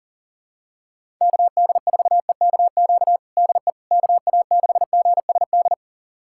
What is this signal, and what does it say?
Morse code, heard as a keyed tone, about 40 wpm, 700 Hz. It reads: KD4EKQ DE KA6GSD
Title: USA Callsign DE Callsign - 6 Characters - 40wpm